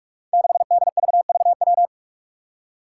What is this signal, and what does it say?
Morse code, heard as a keyed tone, about 45 wpm, 700 Hz. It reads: 6DVVW
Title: Five Challenging Letter-Number Combinations 45wpm